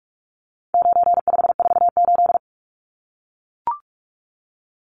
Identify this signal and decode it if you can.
Morse code, heard as a keyed tone, about 44 wpm, 700 Hz. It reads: 9548
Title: Numbers - 4-Digit 45wpm